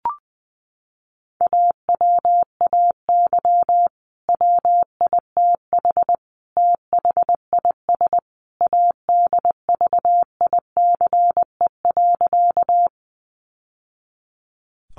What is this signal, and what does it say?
Morse code, heard as a keyed tone, about 20 wpm, 700 Hz. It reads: AWAY WITH THIS ADVICE.
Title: Sentences from Top 1100 Words 20wpm